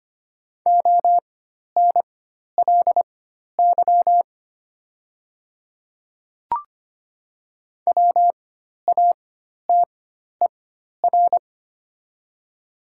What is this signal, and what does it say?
Morse code, heard as a keyed tone, about 25 wpm, spaced as for 12 wpm, 700 Hz. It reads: ONLY WATER
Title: 4x Character Spacing - Top 200 Words - 25wpm